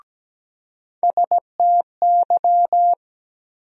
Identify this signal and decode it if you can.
Morse code, heard as a keyed tone, about 17 wpm, 700 Hz. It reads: STY
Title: USA Call Sign Postfixes - 3 Letters 17wpm